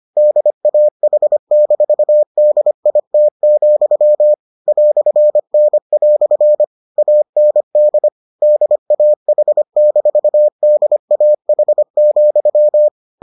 Morse code, heard as a keyed tone, about 25 wpm, 600 Hz. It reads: DAH-DIT, "N" AND DAH-DAH,